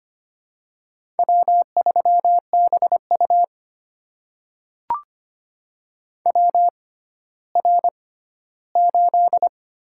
Morse code, heard as a keyed tone, about 25 wpm, spaced as for 14 wpm, 700 Hz. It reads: W3BU WR8
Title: USA Call Signs - 4 Characters - 6x Character Spacing - 25wpm